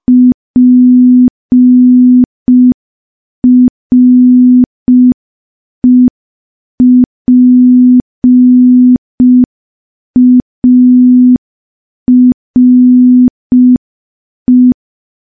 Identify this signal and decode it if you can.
Morse code, heard as a keyed tone, about 5 wpm, 250 Hz. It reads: PREPARE
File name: morse / morse-training-05WPM